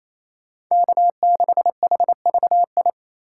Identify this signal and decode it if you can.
Morse code, heard as a keyed tone, about 28 wpm, 700 Hz. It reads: K6HVI
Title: USA Call Signs - 5 Characters - 2x Character Spacing - 28wpm